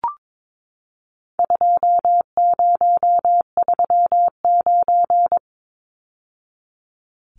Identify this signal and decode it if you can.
Morse code, heard as a keyed tone, about 22 wpm, 700 Hz. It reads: 2039
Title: Numbers - 4-Digit 22wpm